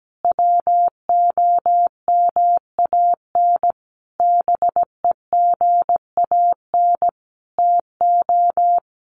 Morse code, heard as a keyed tone, about 17 wpm, 700 Hz. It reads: WOMAN BEGAN TO
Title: Sentences from Top 1100 Words 17wpm